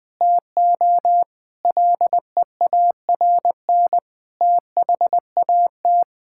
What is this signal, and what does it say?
Morse code, heard as a keyed tone, about 20 wpm, 700 Hz. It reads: TO LEARN THAT
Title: Sentences from Top 1000 Words - Review Part 6 - 20wpm